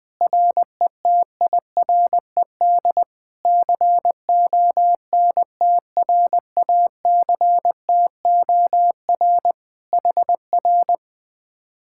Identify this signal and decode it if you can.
Morse code, heard as a keyed tone, about 20 wpm, 700 Hz. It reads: RETIRED CONTRACTOR HR
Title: Speed-Racing Common QSO Phrases - 17wpm